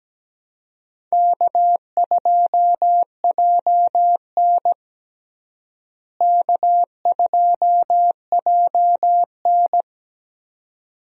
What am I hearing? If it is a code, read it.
Morse code, heard as a keyed tone, about 17 wpm, 700 Hz. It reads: K2JN K2JN
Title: USA Call Signs - 4 Characters - Repeated 2x - 17wpm